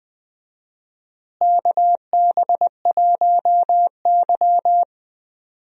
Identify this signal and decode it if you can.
Morse code, heard as a keyed tone, about 20 wpm, 700 Hz. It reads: KB1Y
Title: USA Call Signs - 4 Characters - Repeated 3x - 20wpm